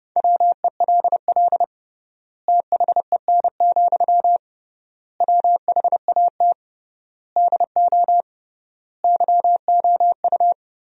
Morse code, heard as a keyed tone, about 31 wpm, 700 Hz. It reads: WELL THEN, WHAT DO YOU
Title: 3x Word-Spacing - Sentences from Top 200 Words - 30wpm